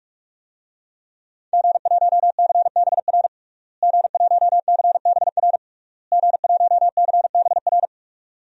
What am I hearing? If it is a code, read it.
Morse code, heard as a keyed tone, about 46 wpm, 700 Hz. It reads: G1CBR G1CBR G1CBR
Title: DXCC-100 Call Signs - 5 Characters - Repeated 3x - 45wpm